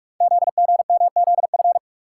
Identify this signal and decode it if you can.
Morse code, heard as a keyed tone, about 44 wpm, 700 Hz. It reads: ZGMZF
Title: Code Groups - 5 Letters - 45wpm